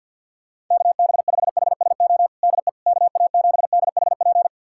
Morse code, heard as a keyed tone, about 45 wpm, 700 Hz. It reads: KB5HSO DE KA7DHP
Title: USA Callsign DE Callsign 50wpm